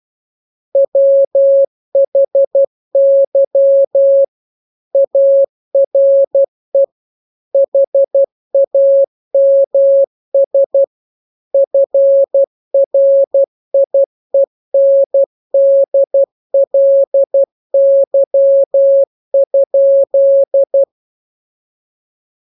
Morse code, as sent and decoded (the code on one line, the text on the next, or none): .-- .... -.-- / .- .-. / .... .- -- ... / ..-. .-. .. -. -.. .-.. -.-- ..--..
WHY ARE HAMS FRIENDLY?